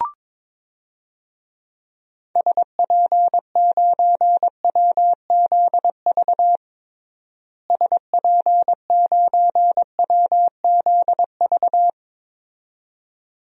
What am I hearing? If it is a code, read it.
Morse code, heard as a keyed tone, about 21 wpm, 700 Hz. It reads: SP9WZV SP9WZV
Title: DXCC-100 Call Signs - 6 Characters - Repeated 2x - 22wpm